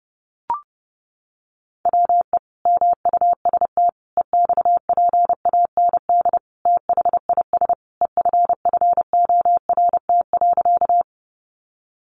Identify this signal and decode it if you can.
Morse code, heard as a keyed tone, about 30 wpm, 700 Hz. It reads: WE MUST EXPAND THIS EFFORT.
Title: Sentences from Top 1400 Words 30wpm